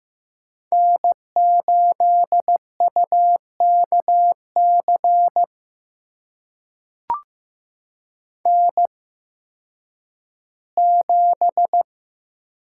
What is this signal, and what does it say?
Morse code, heard as a keyed tone, about 15 wpm, spaced as for 10 wpm, 700 Hz. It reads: N8UKC N7
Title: USA Call Signs - 8x Character Spacing - 15wpm